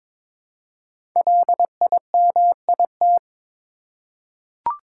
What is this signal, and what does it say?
Morse code, heard as a keyed tone, about 23 wpm, 700 Hz. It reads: LIMIT
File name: 5 Letter Words - First 3 Letters Given - 22wpm